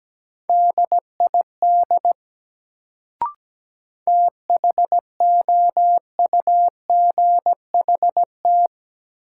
Morse code, heard as a keyed tone, about 17 wpm, 700 Hz. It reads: DID THOUGHT
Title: Sets of 2 Words - New Word Drill - 17wpm